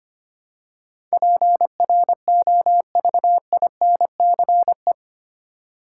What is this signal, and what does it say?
Morse code, heard as a keyed tone, about 25 wpm, 700 Hz. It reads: PROVINCE
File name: 8 Letter Words - First 3 Letters Given - 25wpm